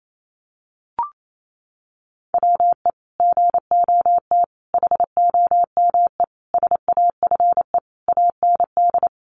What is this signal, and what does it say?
Morse code, heard as a keyed tone, about 28 wpm, 700 Hz. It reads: WE GOT HOME SAFE AND
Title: Sentences from Top 900 Words 28wpm